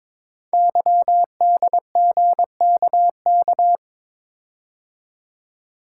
Morse code, heard as a keyed tone, about 22 wpm, 700 Hz. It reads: YDGKK
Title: Code Groups - 5 Letters - 22wpm